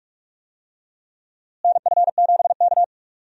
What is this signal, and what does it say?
Morse code, heard as a keyed tone, about 46 wpm, 700 Hz. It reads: NF7K
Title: USA Call Signs - 4 Characters - Repeated 2x - 45wpm